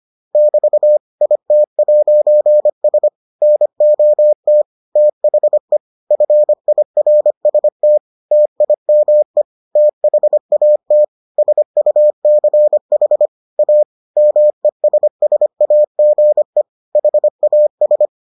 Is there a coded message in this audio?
Yes: Morse code from 0.3 to 18.1 s, heard as a keyed tone, about 25 wpm, 600 Hz. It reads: <BT> IT'S NOT THE FIRST TIME THAT SUCH A MESSAGE HAS